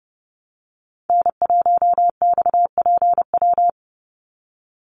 Morse code, heard as a keyed tone, about 30 wpm, 700 Hz. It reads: N1XPW
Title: USA Call Signs 30wpm